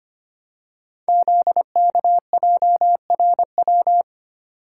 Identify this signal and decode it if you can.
Morse code, heard as a keyed tone, about 25 wpm, 700 Hz. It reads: ZKJRW
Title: Code Groups - 5 Letter:Numbers - 25wpm